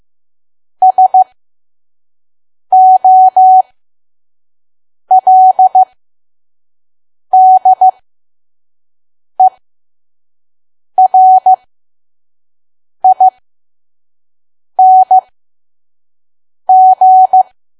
Morse code, heard as a keyed tone, about 15 wpm, spaced as for 5 wpm, 750 Hz. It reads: SOLDERING
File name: Archive / 5 WPM files / PM